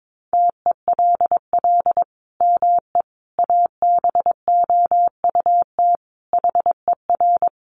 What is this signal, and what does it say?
Morse code, heard as a keyed tone, about 22 wpm, 700 Hz. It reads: TELL ME ABOUT HER
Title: Sentences from Top 1300 Words 22wpm